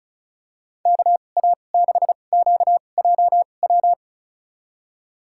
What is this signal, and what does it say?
Morse code, heard as a keyed tone, about 34 wpm, spaced as for 25 wpm, 700 Hz. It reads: KA6QJW
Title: USA Call Signs - 2x Character Spacing - 35wpm